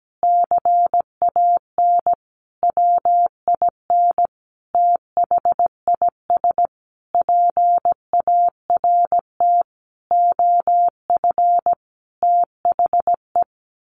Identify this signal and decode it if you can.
Morse code, heard as a keyed tone, about 17 wpm, 700 Hz. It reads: CAN WIN THIS PART OF THE
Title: Sentences from Top 1500 Words 17wpm